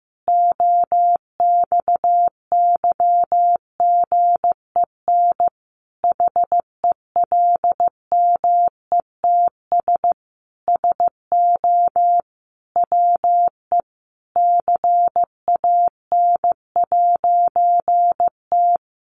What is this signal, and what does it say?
Morse code, heard as a keyed tone, about 15 wpm, 700 Hz. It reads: OXYGEN HELMETS SO WE CAN'T
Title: A Sound Of Thunder 15wpm